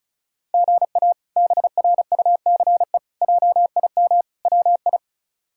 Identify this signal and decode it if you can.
Morse code, heard as a keyed tone, about 36 wpm, 700 Hz. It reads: GA BRUCE JIM WI
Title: K1USN SST QSO Practice - 35wpm